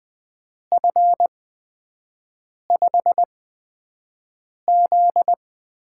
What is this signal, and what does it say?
Morse code, heard as a keyed tone, about 20 wpm, spaced as for 5 wpm, 700 Hz. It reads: F5Z
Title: USA Call Signs - 6 Characters - 8x Character Spacing - 20wpm